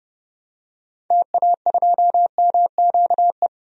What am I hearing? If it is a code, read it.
Morse code, heard as a keyed tone, about 31 wpm, 700 Hz. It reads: TA2MQE
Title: DXCC-100 Call Signs - 6 Characters - 30wpm